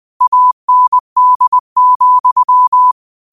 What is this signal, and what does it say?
Morse code, heard as a keyed tone, about 20 wpm, 1000 Hz. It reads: AND,